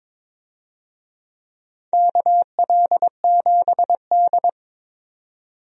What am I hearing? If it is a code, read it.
Morse code, heard as a keyed tone, about 21 wpm, 700 Hz. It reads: KL7D
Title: DXCC-100 Call Signs - 4 Characters - 22wpm